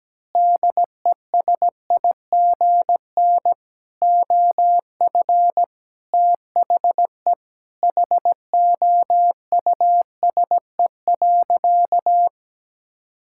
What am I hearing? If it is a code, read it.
Morse code, heard as a keyed tone, about 17 wpm, 700 Hz. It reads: DESIGN OF THE HOUSE.
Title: Sentences from Top 1000 Words - Review Part 6 - 17wpm